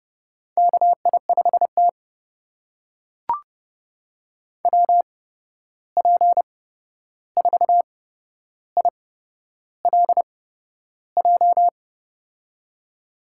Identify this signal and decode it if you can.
Morse code, heard as a keyed tone, about 30 wpm, spaced as for 11 wpm, 700 Hz. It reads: KI5T WP4ILJ
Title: USA Call Signs - 8x Character Spacing - 30wpm